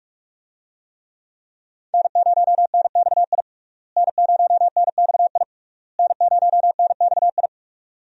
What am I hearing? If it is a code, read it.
Morse code, heard as a keyed tone, about 46 wpm, 700 Hz. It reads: N0NXI N0NXI N0NXI
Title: USA Call Signs - 5 Characters - Repeated 3x - 45wpm